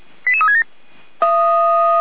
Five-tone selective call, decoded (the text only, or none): 88367